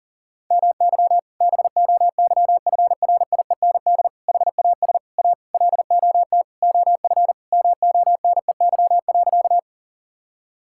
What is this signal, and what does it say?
Morse code, heard as a keyed tone, about 37 wpm, 700 Hz. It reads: MY BOYFRIEND HAS A LOT OF MONEY.
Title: Sentences from Top 1000 Words - Review Part 5 - 40wpm